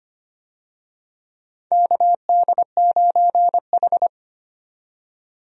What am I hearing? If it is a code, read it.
Morse code, heard as a keyed tone, about 25 wpm, 700 Hz. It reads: KD9H
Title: USA Call Signs - 4 Characters - 25wpm